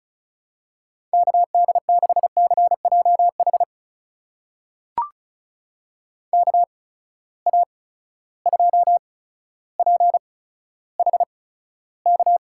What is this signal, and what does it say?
Morse code, heard as a keyed tone, about 34 wpm, spaced as for 15 wpm, 700 Hz. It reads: KD6CJH KA2PHK
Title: USA Call Signs - 6 Characters - 8x Character Spacing - 35wpm